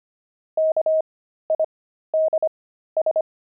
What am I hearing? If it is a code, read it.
Morse code, heard as a keyed tone, about 25 wpm, spaced as for 13 wpm, 630 Hz. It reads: KIDS
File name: pangram-fs-13-cs-25-linoleum